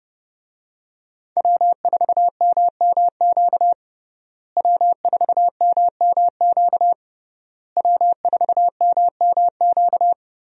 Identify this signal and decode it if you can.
Morse code, heard as a keyed tone, about 32 wpm, 700 Hz. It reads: W4MMQ W4MMQ W4MMQ
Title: USA Call Signs - 5 Characters - Repeated 3x - 30wpm